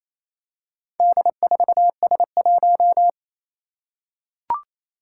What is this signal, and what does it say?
Morse code, heard as a keyed tone, about 27 wpm, 700 Hz. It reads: D4S1
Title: Four Challenging Letter-Number Combinations 28wpm